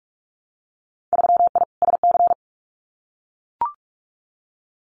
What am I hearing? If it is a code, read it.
Morse code, heard as a keyed tone, about 44 wpm, 700 Hz. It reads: 3I SC
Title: Winter Field Day Exchanges 45wpm